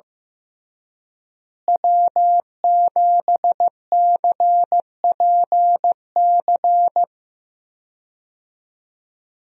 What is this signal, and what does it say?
Morse code, heard as a keyed tone, about 16 wpm, 700 Hz. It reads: W7CPC
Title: USA Call Signs - 5 Characters - Repeated 2x - 15wpm